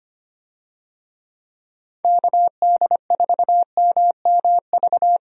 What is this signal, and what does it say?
Morse code, heard as a keyed tone, about 25 wpm, 700 Hz. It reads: KD4MMV
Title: USA Call Signs - 6 Characters - Repeated 3x - 25wpm